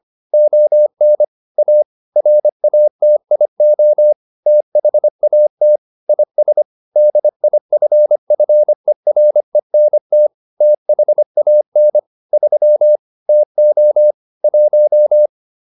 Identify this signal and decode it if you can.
Morse code, heard as a keyed tone, about 25 wpm, 600 Hz. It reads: ON A RATIO THAT IS DIFFERENT THAN 3 TO 1